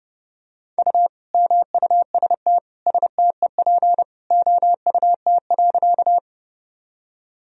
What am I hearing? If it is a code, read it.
Morse code, heard as a keyed tone, about 30 wpm, 700 Hz. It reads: U MUST STEP OUT.
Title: Sentences from Top 400 Words and Abbreviations 30wpm